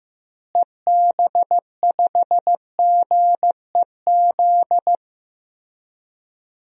Morse code, heard as a keyed tone, about 15 wpm, 700 Hz. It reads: EB5GEZ
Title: DXCC-100 Call Signs - 6 Characters - Repeated 2x - 15wpm